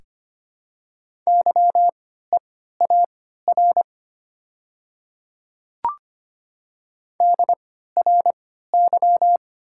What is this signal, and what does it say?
Morse code, heard as a keyed tone, about 25 wpm, spaced as for 14 wpm, 690 Hz. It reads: YEAR DRY
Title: 3x Character Spacing - Top 500 Words - 25wpm